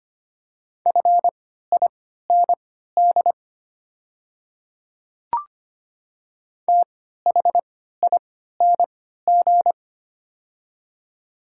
Morse code, heard as a keyed tone, about 25 wpm, spaced as for 14 wpm, 700 Hz. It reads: FIND THING
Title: 3x Character Spacing - Top 100 Words - 25wpm